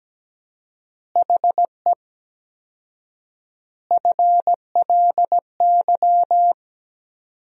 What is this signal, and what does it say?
Morse code, heard as a keyed tone, about 17 wpm, 700 Hz. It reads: HE FLY
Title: 4x Word-Spacing - Sentences from Top 500 Words - 17wpm